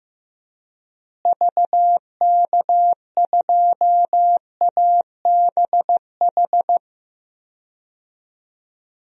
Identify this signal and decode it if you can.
Morse code, heard as a keyed tone, about 15 wpm, 700 Hz. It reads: VK2ABH
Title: DXCC-100 Call Signs - 6 Characters - Repeated 2x - 15wpm